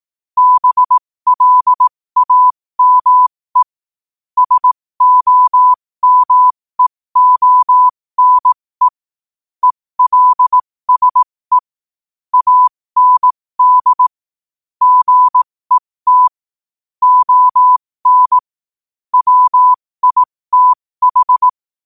Morse code, heard as a keyed tone, about 18 wpm, spaced as for 15 wpm, 1000 Hz. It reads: BLAME SOMEONE ELSE AND GET ON WITH
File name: qotdfiles / 15wpm